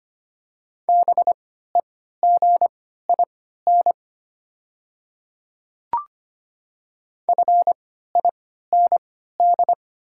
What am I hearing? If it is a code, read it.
Morse code, heard as a keyed tone, about 25 wpm, spaced as for 14 wpm, 700 Hz. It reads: BEGIN FIND
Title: 3x Character Spacing - Top 300 Words - 25wpm